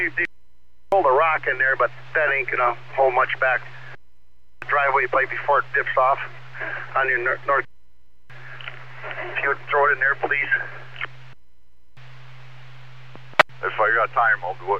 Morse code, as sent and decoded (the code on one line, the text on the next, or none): none